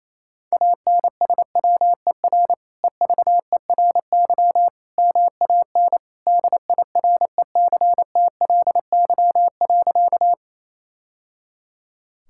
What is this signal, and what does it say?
Morse code, heard as a keyed tone, about 28 wpm, 700 Hz. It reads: ANSWER EVERY MAN DIRECTLY.
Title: Sentences from Top 1000 Words - Review Part 5 - 28wpm